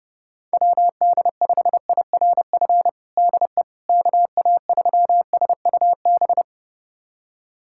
Morse code, heard as a keyed tone, about 32 wpm, 700 Hz. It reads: WD5IRF DE KA3SUB
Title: USA Callsign DE Callsign - 6 Characters - 30wpm